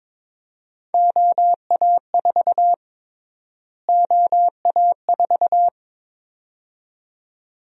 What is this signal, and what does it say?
Morse code, heard as a keyed tone, about 22 wpm, 700 Hz. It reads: OA4 OA4
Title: USA Call Sign Prefixes - Repeated 2x - 22wpm